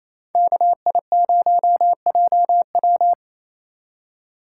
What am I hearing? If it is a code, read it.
Morse code, heard as a keyed tone, about 36 wpm, 700 Hz. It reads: KI0JW